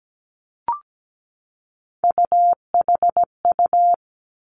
Morse code, heard as a keyed tone, about 17 wpm, 700 Hz. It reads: UHU
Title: Three Challenging Letter-Number Combinations 17wpm